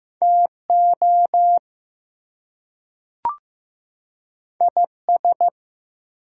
Morse code, heard as a keyed tone, about 15 wpm, 700 Hz. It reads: TO IS
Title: Two Letter Words - 15wpm